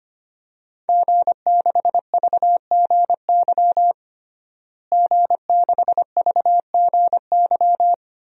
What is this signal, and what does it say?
Morse code, heard as a keyed tone, about 25 wpm, 700 Hz. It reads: G6VGY G6VGY
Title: DXCC-100 Call Signs - 5 Characters - Repeated 2x - 25wpm